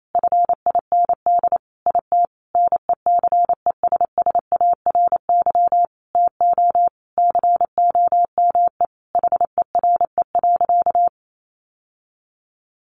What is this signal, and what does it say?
Morse code, heard as a keyed tone, about 28 wpm, 700 Hz. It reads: FIND IT NECESSARY TO COME HERE.
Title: Sentences from Top 1300 Words 28wpm